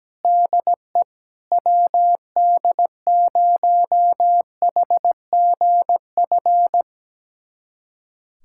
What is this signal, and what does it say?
Morse code, heard as a keyed tone, about 17 wpm, 700 Hz. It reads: DE WD0HGF
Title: USA Callsign DE Callsign - 6 Characters - 17wpm